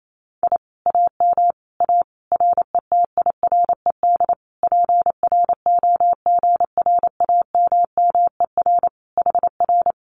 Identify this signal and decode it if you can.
Morse code, heard as a keyed tone, about 28 wpm, 700 Hz. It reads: I AM A RETIRED PROGRAMMER HR
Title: Common QSO Phrases Encore 28wpm